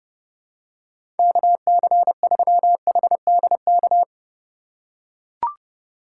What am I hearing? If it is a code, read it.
Morse code, heard as a keyed tone, about 32 wpm, 700 Hz. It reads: KC3HDK
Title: USA Call Signs - 6 Characters - 3x Character Spacing - 30wpm